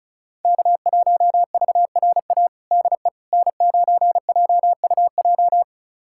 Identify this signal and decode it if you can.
Morse code, heard as a keyed tone, about 34 wpm, 700 Hz. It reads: K1VRA DE N9JUJ
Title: USA Callsign DE Callsign - 5 Characters - 35wpm